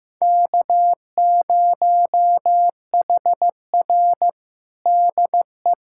Morse code, heard as a keyed tone, about 15 wpm, 700 Hz. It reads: K0HR DE
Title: USA Callsign DE Callsign - 4 Characters - 15wpm